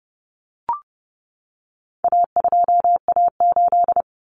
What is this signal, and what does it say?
Morse code, heard as a keyed tone, about 30 wpm, 700 Hz. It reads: A2A8
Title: Common Cut Numbers - 4-Digit 30wpm